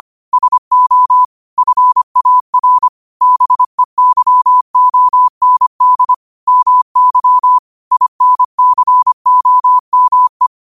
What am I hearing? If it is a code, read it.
Morse code, heard as a keyed tone, about 25 wpm, 1000 Hz. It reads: SO FAR BEYOND MY INCOME